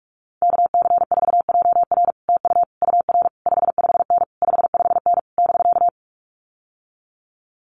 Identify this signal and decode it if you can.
Morse code, heard as a keyed tone, about 46 wpm, 700 Hz. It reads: KC4JR TU UR 55N 55N <BK>
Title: Common QSO Phrases Encore 45wpm